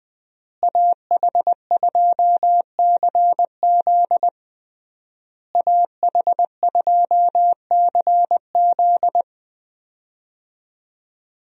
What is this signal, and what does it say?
Morse code, heard as a keyed tone, about 20 wpm, 700 Hz. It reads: AH2CZ AH2CZ
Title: USA Call Signs - 5 Characters - Repeated 3x - 20wpm